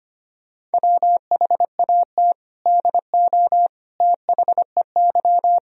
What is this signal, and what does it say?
Morse code, heard as a keyed tone, about 25 wpm, 700 Hz. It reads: WHAT DO THEY
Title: Sentences from Top 100 Words and Abbreviations 25wpm